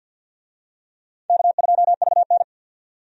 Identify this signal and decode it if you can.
Morse code, heard as a keyed tone, about 52 wpm, 700 Hz. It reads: K2VN
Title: USA Call Signs - 4 Characters - 50wpm